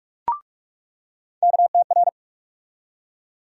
Morse code, heard as a keyed tone, about 42 wpm, 700 Hz. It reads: KTR
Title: USA Call Sign Postfixes - 3 Letters 45wpm